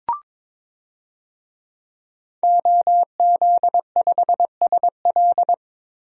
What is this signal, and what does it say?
Morse code, heard as a keyed tone, about 22 wpm, 700 Hz. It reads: OZ5SL
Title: DXCC-100 Call Signs - 5 Characters - Repeated 2x - 22wpm